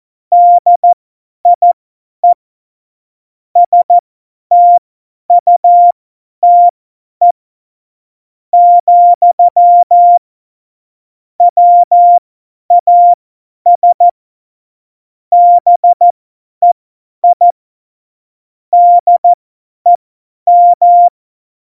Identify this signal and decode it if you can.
Morse code, heard as a keyed tone, about 14 wpm, spaced as for 9 wpm, 700 Hz. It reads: DIE STUTE, WAS BEI DEM
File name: Morsekurs / Witze / Witz 28. August 2018, 50 F70